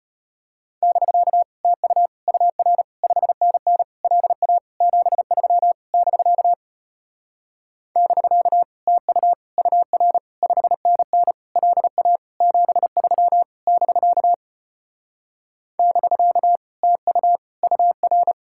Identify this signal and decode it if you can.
Morse code, heard as a keyed tone, about 33 wpm, 700 Hz. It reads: <BK> TU UR 5NN LA 73 <BK> <BK> TU UR 5NN LA 73 <BK> <BK> TU UR